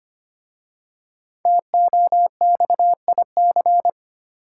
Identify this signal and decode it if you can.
Morse code, heard as a keyed tone, about 25 wpm, 700 Hz. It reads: TOXIC
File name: 5 Letter Words - First 3 Letters Given - 25wpm